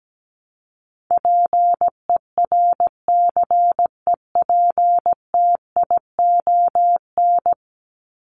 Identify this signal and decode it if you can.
Morse code, heard as a keyed tone, about 17 wpm, 700 Hz. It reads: PERCEPTION
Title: Top 1751-2000 Words 17wpm